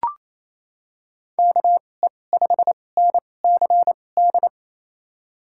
Morse code, heard as a keyed tone, about 28 wpm, spaced as for 20 wpm, 700 Hz. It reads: KE5NCD
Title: USA Call Signs - 6 Characters - 2x Character Spacing - 28wpm